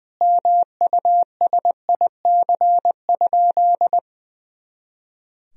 Morse code, heard as a keyed tone, about 19 wpm, 700 Hz. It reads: MUSIC?
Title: Sentences from Top 300 Words and Abbreviations 20wpm